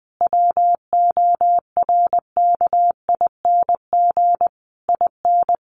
Morse code, heard as a keyed tone, about 20 wpm, 700 Hz. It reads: WORKING IN
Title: Sentences from Top 1200 Words 20wpm